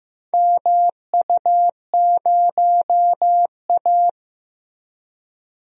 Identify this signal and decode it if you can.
Morse code, heard as a keyed tone, about 15 wpm, 700 Hz. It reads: MU0A
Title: DXCC-100 Call Signs - 4 Characters - Repeated 2x - 15wpm